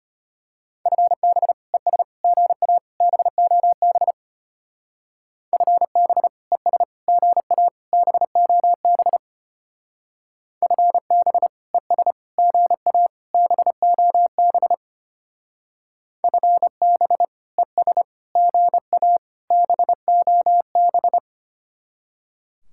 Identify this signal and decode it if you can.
Morse code, heard as a keyed tone, about 30 wpm, 700 Hz. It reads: FB ES GA BOB FB ES GA BOB FB ES GA BOB FB ES GA BOB
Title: Speed-Racing Common QSO Phrases - 25wpm